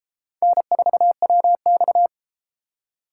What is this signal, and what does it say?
Morse code, heard as a keyed tone, about 33 wpm, 700 Hz. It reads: N4WX